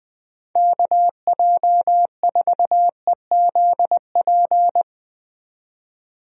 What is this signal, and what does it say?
Morse code, heard as a keyed tone, about 20 wpm, 700 Hz. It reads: KJ4EZP
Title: USA Call Signs - 6 Characters - 20wpm